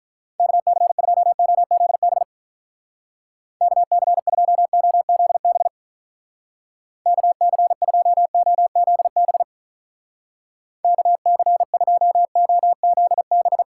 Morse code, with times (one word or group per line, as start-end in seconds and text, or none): 0.4-2.2 KC2OZB
3.6-5.7 KC2OZB
7.1-9.4 KC2OZB
10.8-13.6 KC2OZB